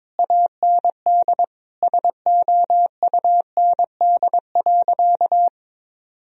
Morse code, heard as a keyed tone, about 22 wpm, 700 Hz. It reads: AND SOUND.
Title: Sentences from Top 1000 Words - Review Part 2 - 22wpm